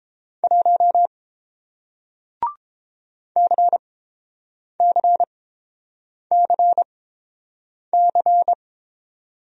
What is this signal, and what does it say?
Morse code, heard as a keyed tone, about 30 wpm, spaced as for 7 wpm, 700 Hz. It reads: 1 CCCC